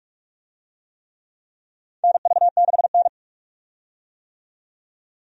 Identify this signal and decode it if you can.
Morse code, heard as a keyed tone, about 44 wpm, 700 Hz. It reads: NV6N
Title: USA Call Signs - 4 Characters - 45wpm